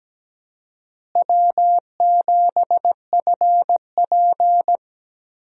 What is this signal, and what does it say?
Morse code, heard as a keyed tone, about 17 wpm, 700 Hz. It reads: W7FP
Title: USA Call Signs - 4 Characters - Repeated 3x - 17wpm